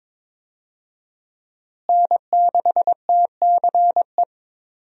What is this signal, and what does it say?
Morse code, heard as a keyed tone, about 21 wpm, 700 Hz. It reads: N6TCE
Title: USA Call Signs - 5 Characters - Repeated 3x - 22wpm